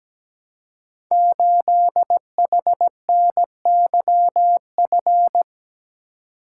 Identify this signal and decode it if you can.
Morse code, heard as a keyed tone, about 17 wpm, 700 Hz. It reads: 8HNYF
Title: Code Groups - 5 Letter:Numbers - 17wpm